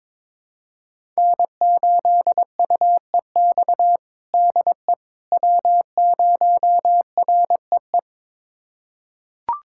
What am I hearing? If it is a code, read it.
Morse code, heard as a keyed tone, about 22 wpm, 700 Hz. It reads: N8UEX DE W0REE
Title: USA Callsign DE Callsign - 5 Characters - 22wpm